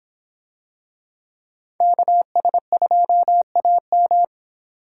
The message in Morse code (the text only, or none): KS2AM